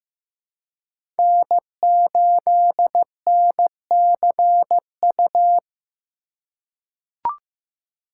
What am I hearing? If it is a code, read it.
Morse code, heard as a keyed tone, about 15 wpm, 700 Hz. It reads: N8NCU
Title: USA Call Signs - 5 Characters - 3x Character Spacing - 15wpm